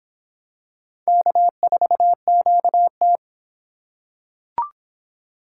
Morse code, heard as a keyed tone, about 23 wpm, 700 Hz. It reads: K4QT